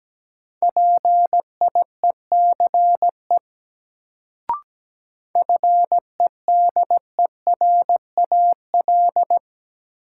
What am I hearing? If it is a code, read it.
Morse code, heard as a keyed tone, about 17 wpm, 700 Hz. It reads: PIECE FEDERAL
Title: Sets of 3 Words - New Word Drill - 17wpm